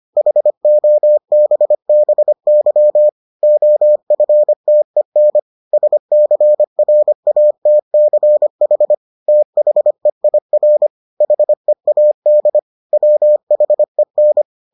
Morse code, heard as a keyed tone, about 25 wpm, 600 Hz. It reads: HOBBY OFTEN SCRATCH THEIR HEAD WHEN